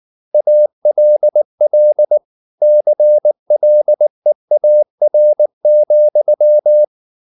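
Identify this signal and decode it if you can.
Morse code, heard as a keyed tone, about 19 wpm, 600 Hz. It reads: ALL CLEAR,
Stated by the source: Morse code at 19 wpm from CW med Gnister, reading Ken Follett's novel Never in English.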